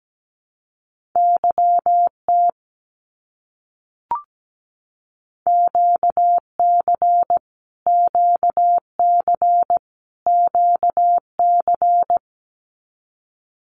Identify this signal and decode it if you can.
Morse code, heard as a keyed tone, about 17 wpm, 700 Hz. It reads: YT QC QC QC
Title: Canadian Provinces and Territories - Repeated 3x - 17wpm